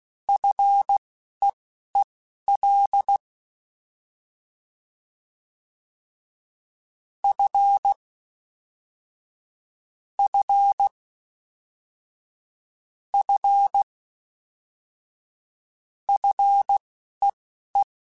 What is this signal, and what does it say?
Morse code, heard as a keyed tone, about 16 wpm, spaced as for 11 wpm, 800 Hz. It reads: FEEL F F F FEE
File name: F and L / F and L mp3 / F and L Prefixes 3x2 Twice @10